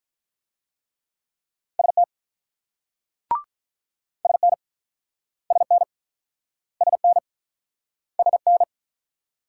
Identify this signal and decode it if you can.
Morse code, heard as a keyed tone, about 46 wpm, 700 Hz. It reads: ST SN SN SN SN